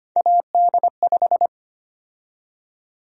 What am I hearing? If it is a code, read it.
Morse code, heard as a keyed tone, about 25 wpm, 700 Hz. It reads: AD5
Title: USA Call Sign Prefixes - Repeated 2x - 25wpm